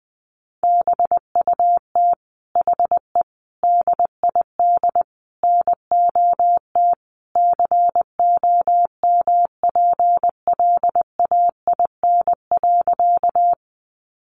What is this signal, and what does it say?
Morse code, heard as a keyed tone, about 20 wpm, 700 Hz. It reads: BUT HE DID NOT COMPLAIN.
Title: Sentences from Top 1300 Words 20wpm